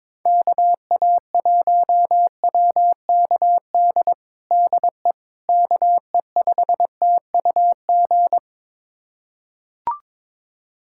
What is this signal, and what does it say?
Morse code, heard as a keyed tone, about 22 wpm, 700 Hz. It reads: KA1WKD DE KE5TUG
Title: USA Callsign DE Callsign - 6 Characters - 22wpm